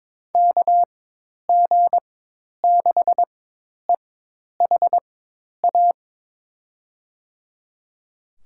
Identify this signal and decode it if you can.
Morse code, heard as a keyed tone, about 21 wpm, spaced as for 10 wpm, 700 Hz. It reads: KG6EHA
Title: USA Call Signs - 6 Characters - 4x Character Spacing - 22wpm